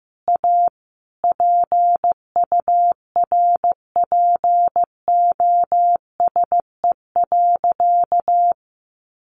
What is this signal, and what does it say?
Morse code, heard as a keyed tone, about 15 wpm, 700 Hz. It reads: A PURPOSE.
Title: Sentences from Top 800 Words 15wpm